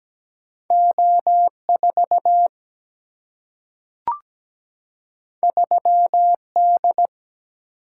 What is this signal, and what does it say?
Morse code, heard as a keyed tone, about 17 wpm, 700 Hz. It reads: O4 3D
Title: Single-Letter Single-Number 17wpm